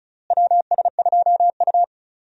Morse code, heard as a keyed tone, about 36 wpm, 700 Hz. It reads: WS2U
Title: USA Call Signs - 4 Characters - Repeated 2x - 35wpm